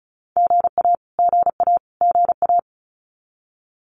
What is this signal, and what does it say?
Morse code, heard as a keyed tone, about 34 wpm, 700 Hz. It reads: GA GA GA
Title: Abbreviations Repeated 3x - 35wpm